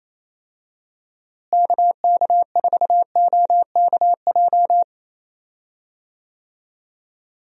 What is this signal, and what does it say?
Morse code, heard as a keyed tone, about 27 wpm, 700 Hz. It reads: KK4OKJ
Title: USA Call Signs - 6 Characters - Repeated 2x - 28wpm